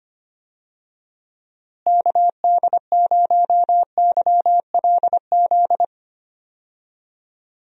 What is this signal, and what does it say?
Morse code, heard as a keyed tone, about 25 wpm, 700 Hz. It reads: KD0YLZ
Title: USA Call Signs - 6 Characters - Repeated 2x - 25wpm